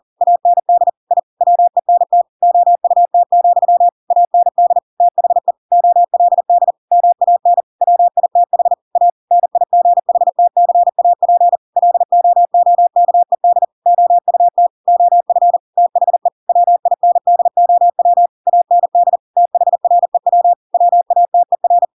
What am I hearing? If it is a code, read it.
Morse code, heard as a keyed tone, about 40 wpm, 700 Hz. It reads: AND I WENT OUT, AND THE OLD MAN WITH A NIGHTCAP LOOKED OUT OF THE WINDOW AND THREW WATER
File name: Pinocchio / Pinocchio ch7 40 40 ik1hge